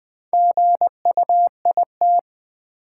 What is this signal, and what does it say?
Morse code, heard as a keyed tone, about 20 wpm, 700 Hz. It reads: GUIT
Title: Build A Word - 2 Letters At A Time - 6-Letter Words - 20wpm